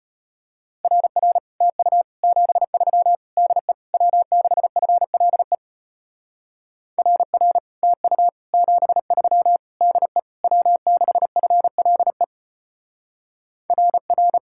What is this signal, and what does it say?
Morse code, heard as a keyed tone, about 30 wpm, 700 Hz. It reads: RR TU 73 DE W6FLE RR TU 73 DE W6FLE RR
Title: Speed-Racing Common QSO Phrases - 25wpm